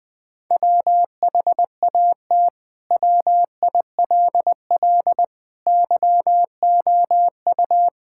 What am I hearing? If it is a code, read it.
Morse code, heard as a keyed tone, about 20 wpm, 700 Hz. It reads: WHAT WILL YOU
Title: Sentences from Top 1000 Words - Review Part 6 - 20wpm